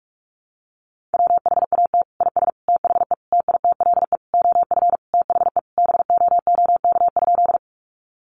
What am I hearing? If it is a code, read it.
Morse code, heard as a keyed tone, about 44 wpm, 700 Hz. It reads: WHAT IS THE TITLE OF THE BOOK?
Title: Sentences from Top 1200 Words 45wpm